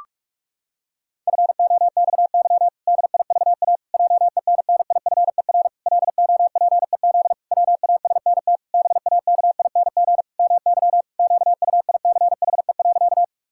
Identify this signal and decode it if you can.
Morse code, heard as a keyed tone, about 44 wpm, 700 Hz. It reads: FOXY DIVA JENNIFER LOPEZ WASNT BAKING MY QUICHE.
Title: Pangram Challenge v2 - 45wpm